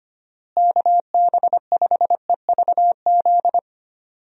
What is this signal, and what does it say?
Morse code, heard as a keyed tone, about 25 wpm, 700 Hz. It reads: KB5EVZ
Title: USA Call Signs - 6 Characters - Repeated 2x - 25wpm